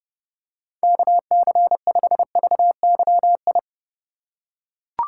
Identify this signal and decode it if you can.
Morse code, heard as a keyed tone, about 30 wpm, 700 Hz. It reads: KC5VYI
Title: USA Call Signs - 6 Characters - 3x Character Spacing - 30wpm